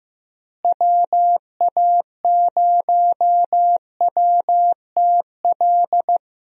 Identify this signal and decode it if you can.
Morse code, heard as a keyed tone, about 15 wpm, 700 Hz. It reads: WA0WTL
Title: USA Call Signs - 6 Characters - Repeated 2x - 15wpm